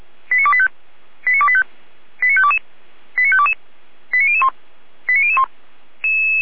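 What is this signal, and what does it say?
Five-tone selective call (ZVEI), heard as quick runs of five tones, tones of about 70 ms. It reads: 89285 89285 89522 89522 89001 89001
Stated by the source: ZVEI five-tone selective call received by SDR.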